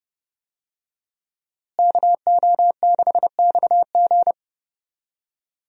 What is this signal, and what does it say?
Morse code, heard as a keyed tone, about 31 wpm, 700 Hz. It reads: KO6XG
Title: USA Call Signs - 5 Characters - Repeated 3x - 30wpm